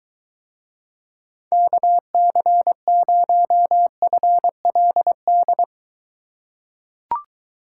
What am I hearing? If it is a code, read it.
Morse code, heard as a keyed tone, about 22 wpm, 700 Hz. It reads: KC0FLD